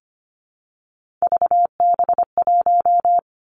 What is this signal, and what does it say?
Morse code, heard as a keyed tone, about 25 wpm, 700 Hz. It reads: VB1
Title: Three Challenging Letter-Number Combinations 25wpm